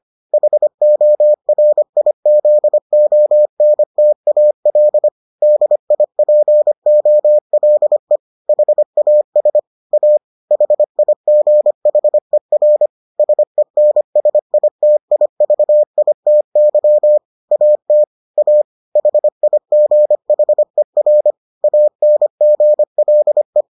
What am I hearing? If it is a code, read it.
Morse code, heard as a keyed tone, about 25 wpm, 600 Hz. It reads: HORIZONTAL DIPOLE HAS A HIGHER SENSITIVITY AT A HIGHER ANGLE